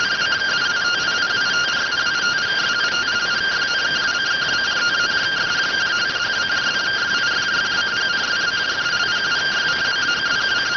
Two-tone teleprinter signal, about 170 Hz shift.